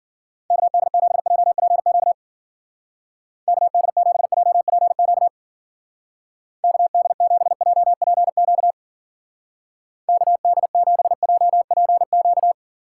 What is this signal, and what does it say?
Morse code, heard as a keyed tone, about 50 wpm, 700 Hz. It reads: KD7JPQ KD7JPQ KD7JPQ KD7JPQ
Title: Speed Racing - USA 2x3 Call Signs 40wpm